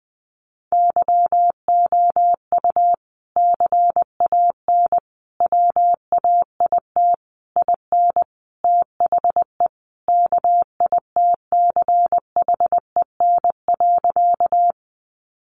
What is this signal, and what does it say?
Morse code, heard as a keyed tone, about 20 wpm, 700 Hz. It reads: YOU CAN WAIT IN THE KITCHEN.
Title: Sentences from Top 1400 Words 20wpm